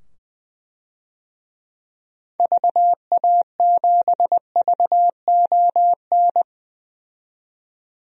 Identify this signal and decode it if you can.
Morse code, heard as a keyed tone, about 20 wpm, 700 Hz. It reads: VA7VON
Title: DXCC-100 Call Signs - 6 Characters - 20wpm